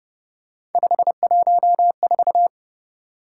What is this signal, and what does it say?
Morse code, heard as a keyed tone, about 31 wpm, 700 Hz. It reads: 514
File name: Speed-Racing - Three Letter-Number Combinations - 25wpm